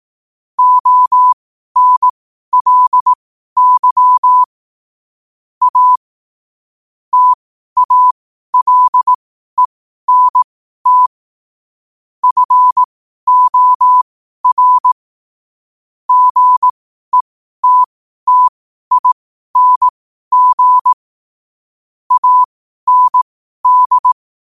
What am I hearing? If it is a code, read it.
Morse code, heard as a keyed tone, about 18 wpm, spaced as for 12 wpm, 1000 Hz. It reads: ONLY A TALENT FOR GETTING AND
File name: qotdfiles / 12wpm